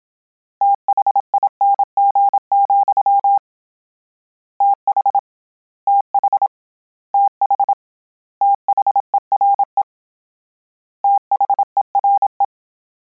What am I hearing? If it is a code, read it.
Morse code, heard as a keyed tone, about 27 wpm, 800 Hz. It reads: THING, TH TH TH THERE THERE